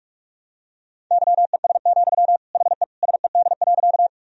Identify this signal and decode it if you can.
Morse code, heard as a keyed tone, about 46 wpm, 690 Hz. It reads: YES, HE SED.
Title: Sentences from Top 500 Words and Abbreviations 45wpm